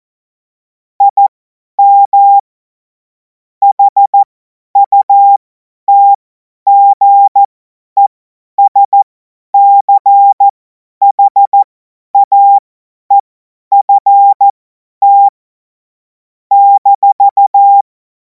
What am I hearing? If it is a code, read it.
Morse code, heard as a keyed tone, about 14 wpm, spaced as for 10 wpm, 800 Hz. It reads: IM HUTGESCHAEFT -
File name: Morsekurs / Witze / Witz 17. März 2026 50 F70